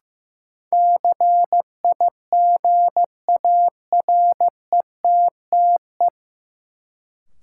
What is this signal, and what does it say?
Morse code, heard as a keyed tone, about 16 wpm, 700 Hz. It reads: CIGARETTE
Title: Sentences from Top 1000 Words - New Word Drill - 15wpm